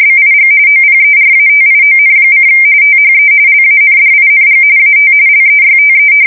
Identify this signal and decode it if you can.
RTTY, about 45.45 baud. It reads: BIG OPERATION: THE LANDING AT LEYTE
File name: fsk170 / battleshipnj